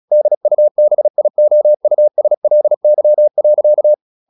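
Morse code, heard as a keyed tone, about 36 wpm, 600 Hz. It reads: DUBIOUSLY.
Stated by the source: Morse practice at 36 wpm, text of Ken Follett's novel Never, from CW med Gnister.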